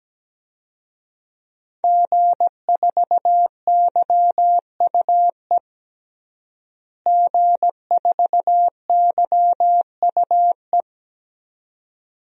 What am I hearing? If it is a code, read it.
Morse code, heard as a keyed tone, about 17 wpm, 700 Hz. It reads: G4YUE G4YUE
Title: DXCC-100 Call Signs - 5 Characters - Repeated 2x - 17wpm